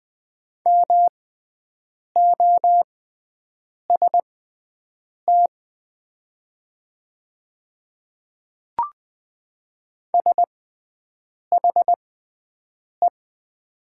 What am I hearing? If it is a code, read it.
Morse code, heard as a keyed tone, about 20 wpm, spaced as for 7 wpm, 700 Hz. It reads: MOST SHE
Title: 6x Character Spacing - Top 100 Words - 20wpm